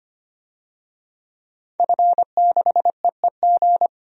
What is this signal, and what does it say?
Morse code, heard as a keyed tone, about 25 wpm, 700 Hz. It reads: F6EEG
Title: DXCC-100 Call Signs - 5 Characters - 25wpm